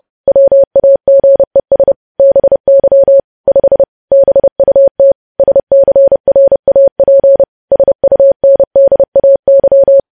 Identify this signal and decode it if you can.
Morse code, heard as a keyed tone, about 30 wpm, 550 Hz. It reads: WAGES BY 5 BUT SCRAP SUNDAY
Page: news headlines in morse code at 30 wpm